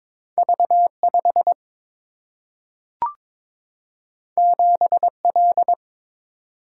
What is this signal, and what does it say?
Morse code, heard as a keyed tone, about 22 wpm, 700 Hz. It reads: V 5 7 L
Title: Single-Letter Single-Number 22wpm